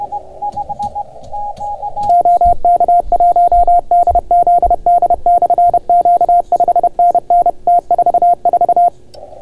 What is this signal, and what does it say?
Morse code, heard as a keyed tone, about 31 wpm, 690 Hz. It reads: OK1DZD/Q 5NN T44